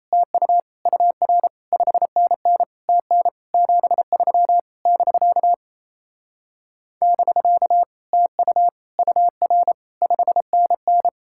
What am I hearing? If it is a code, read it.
Morse code, heard as a keyed tone, about 37 wpm, 700 Hz. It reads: TU UR 5NN TN 73 <BK> <BK> TU UR 5NN